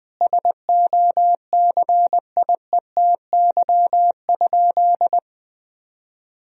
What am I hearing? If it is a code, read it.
Morse code, heard as a keyed tone, about 21 wpm, 700 Hz. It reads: SOCIETY?
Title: Sentences from Top 1000 Words - Review Part 4 - 20wpm